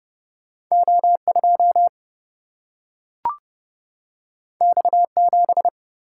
Morse code, heard as a keyed tone, about 30 wpm, 700 Hz. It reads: O2 X7
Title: Single-Letter Single-Number 30wpm